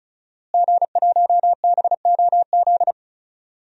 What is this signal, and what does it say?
Morse code, heard as a keyed tone, about 36 wpm, 700 Hz. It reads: G1BOZ
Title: DXCC-100 Call Signs - 5 Characters - Repeated 3x - 35wpm